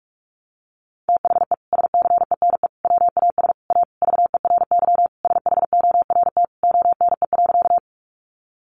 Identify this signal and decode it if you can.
Morse code, heard as a keyed tone, about 44 wpm, 700 Hz. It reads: THE SCENE WAS A VERY SHORT ONE.
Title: Sentences from Top 1100 Words 45wpm